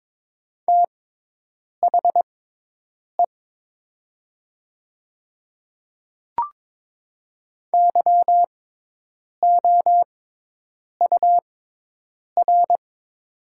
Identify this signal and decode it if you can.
Morse code, heard as a keyed tone, about 22 wpm, spaced as for 8 wpm, 700 Hz. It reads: THE YOUR
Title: 6x Character Spacing - Top 100 Words - 22wpm